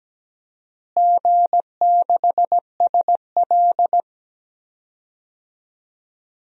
Morse code, heard as a keyed tone, about 17 wpm, 700 Hz. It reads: G6SL
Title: DXCC-100 Call Signs - 4 Characters - Repeated 2x - 17wpm